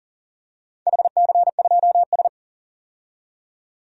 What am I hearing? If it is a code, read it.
Morse code, heard as a keyed tone, about 40 wpm, 700 Hz. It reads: HC2S
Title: DXCC-100 Call Signs - 4 Characters - 40wpm